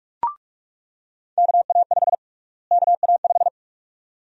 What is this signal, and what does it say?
Morse code, heard as a keyed tone, about 42 wpm, 700 Hz. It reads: KA5 KA5
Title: USA Call Sign Prefixes - Repeated 2x - 45wpm